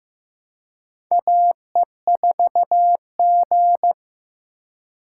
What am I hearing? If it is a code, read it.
Morse code, heard as a keyed tone, about 15 wpm, 700 Hz. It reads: AE4G
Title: USA Call Signs - 4 Characters - 2x Character Spacing - 15wpm